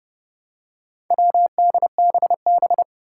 Morse code, heard as a keyed tone, about 31 wpm, 700 Hz. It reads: WDBB
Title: Four Challenging Letter-Number Combinations 30wpm